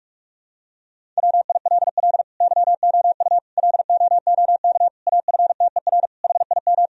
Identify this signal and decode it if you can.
Morse code, heard as a keyed tone, about 48 wpm, 700 Hz. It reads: WILL YOU LOOK AFTER HIM
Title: Sentences from Top 1000 Words - Review Part 1 - 45wpm